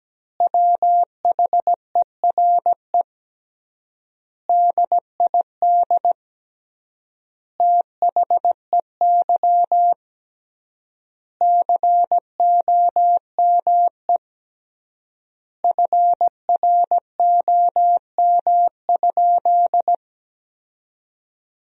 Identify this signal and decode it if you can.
Morse code, heard as a keyed tone, about 17 wpm, 700 Hz. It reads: WHERE DID THEY COME FROM?
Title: 3x Word-Spacing - Sentences from Top 200 Words - 17wpm